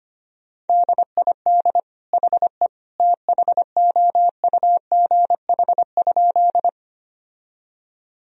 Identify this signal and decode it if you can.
Morse code, heard as a keyed tone, about 25 wpm, 700 Hz. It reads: DID HE THOUGH?
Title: Sentences from Top 1000 Words - Review Part 2 - 25wpm